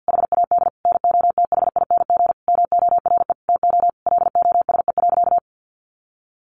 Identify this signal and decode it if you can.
Morse code, heard as a keyed tone, about 49 wpm, 700 Hz. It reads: HAD NOTHING MORE TO LOSE.
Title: Sentences from Top 1100 Words 50wpm